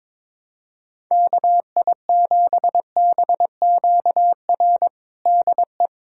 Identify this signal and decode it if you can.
Morse code, heard as a keyed tone, about 22 wpm, 700 Hz. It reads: KI7BQR DE
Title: USA Callsign DE Callsign 22wpm